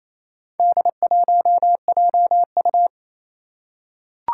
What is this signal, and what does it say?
Morse code, heard as a keyed tone, about 28 wpm, 700 Hz. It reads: D1JU
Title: Four Challenging Letter-Number Combinations 28wpm